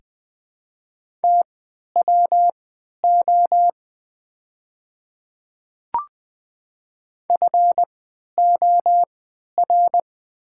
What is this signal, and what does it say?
Morse code, heard as a keyed tone, about 20 wpm, spaced as for 11 wpm, 700 Hz. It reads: TWO FOR
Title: 3x Character Spacing - Top 100 Words - 20wpm